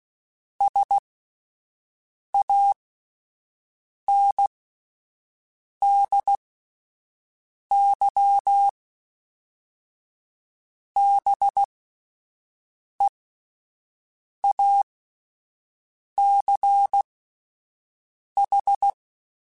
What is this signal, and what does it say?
Morse code, heard as a keyed tone, about 16 wpm, spaced as for 5 wpm, 800 Hz. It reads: SANDY BEACH